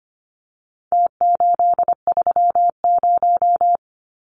Morse code, heard as a keyed tone, about 25 wpm, 700 Hz. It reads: T830
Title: Numbers - Contest Style - 4-Digit 25wpm